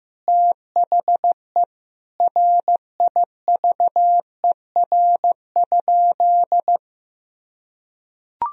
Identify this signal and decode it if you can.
Morse code, heard as a keyed tone, about 15 wpm, 700 Hz. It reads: THE RIVER?
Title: Sentences from Top 1000 Words - Review Part 4 - 15wpm